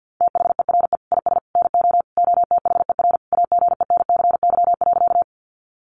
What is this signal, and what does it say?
Morse code, heard as a keyed tone, about 50 wpm, 700 Hz. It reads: THERE IS NO OTHER AGENCY.
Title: Sentences from Top 1300 Words 50wpm